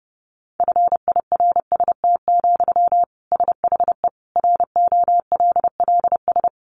Morse code, heard as a keyed tone, about 30 wpm, 700 Hz. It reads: FIRST, SHE ROLLS
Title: Easy Phrases 30wpm